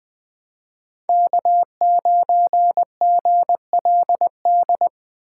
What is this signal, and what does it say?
Morse code, heard as a keyed tone, about 20 wpm, 700 Hz. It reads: K9GLD
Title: Speed Racing - USA 1x3 Call Signs 17wpm